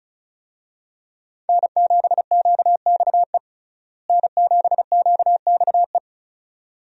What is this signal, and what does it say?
Morse code, heard as a keyed tone, about 34 wpm, 700 Hz. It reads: N7QXE N7QXE
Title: USA Call Signs - 5 Characters - Repeated 2x - 35wpm